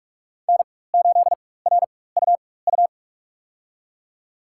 Morse code, heard as a keyed tone, about 43 wpm, spaced as for 21 wpm, 700 Hz. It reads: N8RUU
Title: USA Call Signs - 5 Characters - 4x Character Spacing - 45wpm